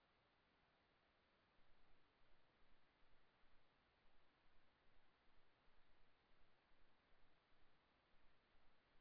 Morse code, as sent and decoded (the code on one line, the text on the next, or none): none